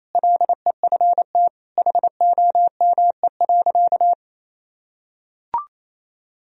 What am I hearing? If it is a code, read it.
Morse code, heard as a keyed tone, about 27 wpm, 700 Hz. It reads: LEFT HOME.
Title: Sentences from Top 1000 Words - Review Part 4 - 28wpm